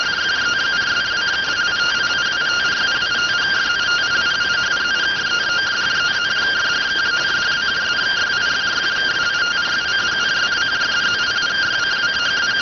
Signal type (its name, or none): two-tone teleprinter signal